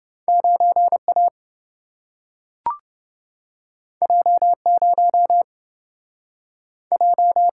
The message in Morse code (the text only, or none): 9A J0 J